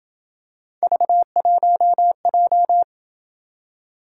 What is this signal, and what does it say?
Morse code, heard as a keyed tone, about 27 wpm, 700 Hz. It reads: V1J